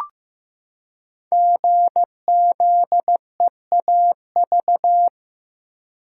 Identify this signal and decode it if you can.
Morse code, heard as a keyed tone, about 15 wpm, 700 Hz. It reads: GZEAV
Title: Code Groups - 5 Letter:Numbers - 15wpm